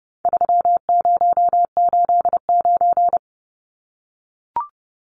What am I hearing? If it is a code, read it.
Morse code, heard as a keyed tone, about 30 wpm, 700 Hz. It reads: 3089
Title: Numbers - 4-Digit 30wpm